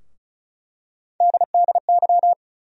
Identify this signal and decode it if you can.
Morse code, heard as a keyed tone, about 36 wpm, 700 Hz. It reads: DDY
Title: USA Call Sign Postfixes - 3 Letters 35wpm